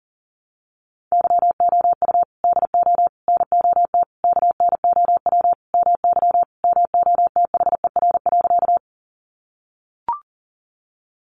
Morse code, heard as a keyed tone, about 40 wpm, 700 Hz. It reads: YOU DO NOT KNOW MY MOTHER.
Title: Sentences from Top 600 Words 40wpm